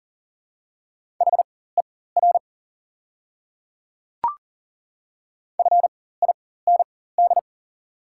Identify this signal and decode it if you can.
Morse code, heard as a keyed tone, about 40 wpm, spaced as for 19 wpm, 700 Hz. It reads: HER FIND
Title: 4x Character Spacing - Top 100 Words - 40wpm